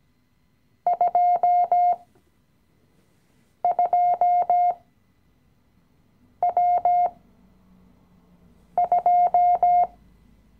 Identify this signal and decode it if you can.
Morse code, heard as a keyed tone, about 17 wpm, spaced as for 5 wpm, 710 Hz. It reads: 22W2
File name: CW-Online-Kurs / Anfänger / Lektion 017